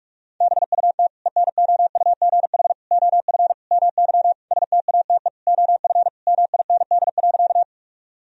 Morse code, heard as a keyed tone, about 46 wpm, 700 Hz. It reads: BUT ENOUGH OF MY STATE OF MIND.
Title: Sentences from Top 1000 Words - Review Part 2 - 45wpm